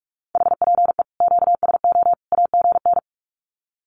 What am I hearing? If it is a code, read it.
Morse code, heard as a keyed tone, about 44 wpm, 700 Hz. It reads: HPE QSO AGN
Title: Common QSO Phrases 45wpm